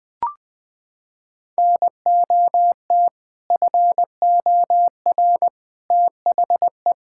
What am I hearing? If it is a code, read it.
Morse code, heard as a keyed tone, about 21 wpm, 700 Hz. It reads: NOT FOR THE
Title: Sets of 6 Words - 20wpm